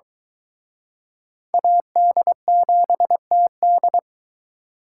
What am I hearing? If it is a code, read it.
Morse code, heard as a keyed tone, about 24 wpm, 700 Hz. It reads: AD7TD